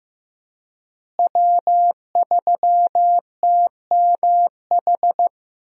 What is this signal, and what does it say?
Morse code, heard as a keyed tone, about 15 wpm, 700 Hz. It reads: W3TMH
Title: USA Call Signs - 5 Characters - Repeated 2x - 15wpm